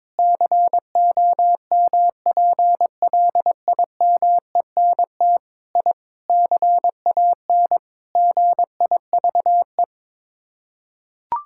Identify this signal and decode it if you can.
Morse code, heard as a keyed tone, about 22 wpm, 700 Hz. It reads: COMPLIMENT I CAN GIVE
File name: practice / 22wpm / Sets of 8 Words - 22wpm